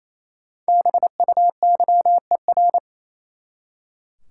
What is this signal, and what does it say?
Morse code, heard as a keyed tone, about 28 wpm, 700 Hz. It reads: BUYER
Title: Five Letter Words - 28wpm